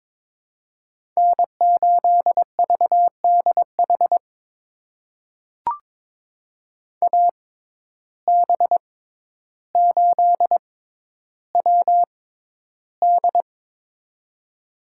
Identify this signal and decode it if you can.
Morse code, heard as a keyed tone, about 22 wpm, spaced as for 11 wpm, 700 Hz. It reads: N8VDH AB8WD
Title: USA Call Signs - 5 Characters - 6x Character Spacing - 22wpm